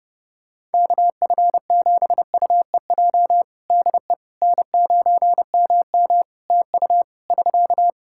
Morse code, heard as a keyed tone, about 32 wpm, 700 Hz. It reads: KF7UEJ DE N9MM TU <SK>
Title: Practice QSO - Exchange 4 - Ending - 30wpm